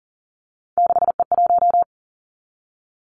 Morse code, heard as a keyed tone, about 42 wpm, 700 Hz. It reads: 6E1
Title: Three Challenging Letter-Number Combinations 40wpm